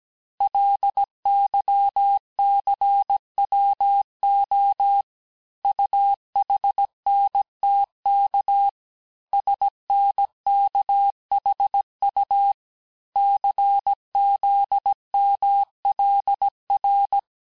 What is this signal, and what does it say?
Morse code, heard as a keyed tone, about 17 wpm, 800 Hz. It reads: LYCWO UHNTK SNKHU CZMLR
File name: cours-du-lundi files / cours-1587-texte-chiffre-vitesse-16MPM